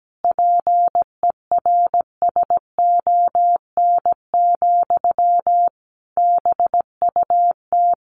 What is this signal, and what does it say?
Morse code, heard as a keyed tone, about 17 wpm, 700 Hz. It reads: PERSON, BUT
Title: Sentences from Top 1400 Words 17wpm